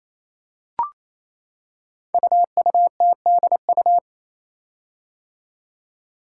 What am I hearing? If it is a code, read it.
Morse code, heard as a keyed tone, about 29 wpm, 700 Hz. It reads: UUTDU
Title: Code Groups - 5 Letters - 28wpm